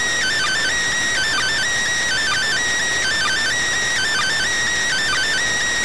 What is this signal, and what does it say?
Emergency Alert System (SAME) tones (header bursts), heard as harsh buzzing bursts. It is heard from 1.1 to 5.3 s.